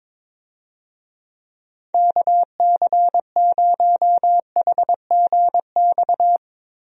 Morse code, heard as a keyed tone, about 22 wpm, 700 Hz. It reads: KC0HGX
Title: USA Call Signs - 6 Characters - Repeated 3x - 22wpm